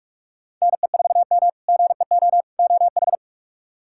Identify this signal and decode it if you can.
Morse code, heard as a keyed tone, about 41 wpm, 700 Hz. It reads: NE4M GEO OH
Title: K1USN SST QSO Practice - 45wpm